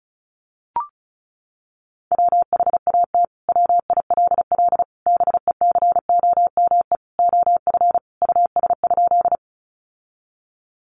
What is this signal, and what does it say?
Morse code, heard as a keyed tone, about 34 wpm, 700 Hz. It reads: WHAT WILL BECOME OF US?
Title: Sentences from Top 600 Words 35wpm